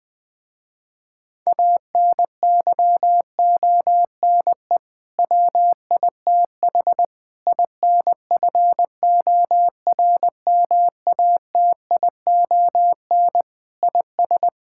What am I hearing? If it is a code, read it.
Morse code, heard as a keyed tone, about 21 wpm, 700 Hz. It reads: ANYONE WITH INFORMATION IS
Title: Sentences with 7-Word Phrases - Spoken Beginning - 20wpm